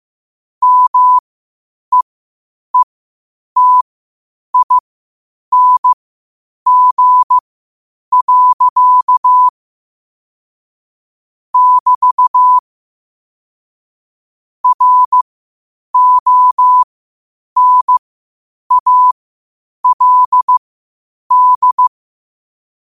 Morse code, heard as a keyed tone, about 15 wpm, spaced as for 8 wpm, 1000 Hz. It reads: MEETING. <BT> RONALD
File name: qotdfiles / 8wpm